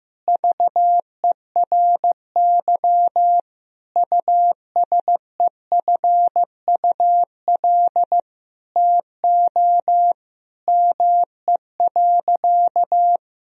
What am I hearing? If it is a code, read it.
Morse code, heard as a keyed tone, about 15 wpm, 700 Hz. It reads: VERY USEFUL TO ME.
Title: Sentences from Top 1000 Words - Review Part 4 - 15wpm